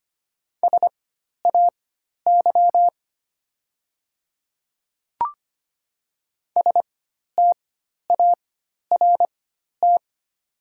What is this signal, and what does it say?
Morse code, heard as a keyed tone, about 25 wpm, spaced as for 12 wpm, 700 Hz. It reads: SAY START
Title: 4x Character Spacing - Top 300 Words - 25wpm